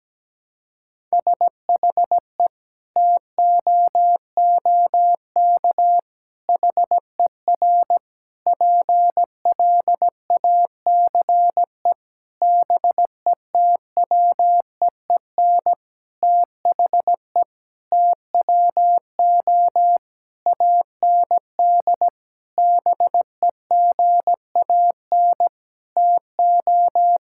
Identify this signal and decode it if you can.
Morse code, heard as a keyed tone, about 17 wpm, 700 Hz. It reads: SHE TOOK HER PLACE BETWEEN THE TWO AND BEGAN TO
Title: Sentences from Top 1000 Words - Review Part 5 - 17wpm